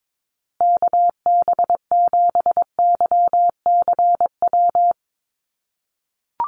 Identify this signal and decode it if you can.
Morse code, heard as a keyed tone, about 22 wpm, 700 Hz. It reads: KB7YCW
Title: USA Call Signs Repeated 2x 22wpm